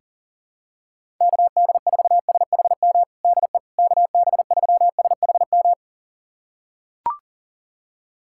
Morse code, heard as a keyed tone, about 40 wpm, 700 Hz. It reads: KD4SHM DE KB3SHM
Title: USA Callsign DE Callsign - 6 Characters - 40wpm